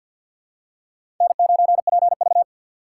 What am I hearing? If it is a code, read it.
Morse code, heard as a keyed tone, about 46 wpm, 700 Hz. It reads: N9PV
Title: USA Call Signs - 4 Characters - 3x Character Spacing - 50wpm